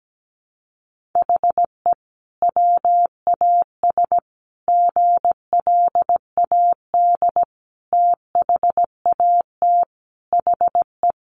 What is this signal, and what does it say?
Morse code, heard as a keyed tone, about 17 wpm, 700 Hz. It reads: HE WAS GLAD THAT HE
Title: Sentences from Top 1400 Words 17wpm